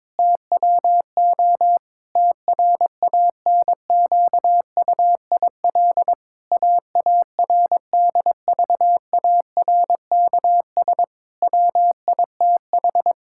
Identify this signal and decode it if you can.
Morse code, heard as a keyed tone, about 22 wpm, 700 Hz. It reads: TWO TRANQUIL AARDVARKS WITH
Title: Pangram Challenge v2 - 22wpm